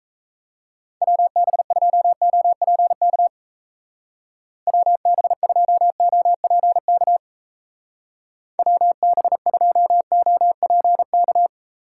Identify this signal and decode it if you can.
Morse code, heard as a keyed tone, about 34 wpm, 700 Hz. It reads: WB2OPK WB2OPK WB2OPK
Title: Speed Racing - USA Call Signs 28wpm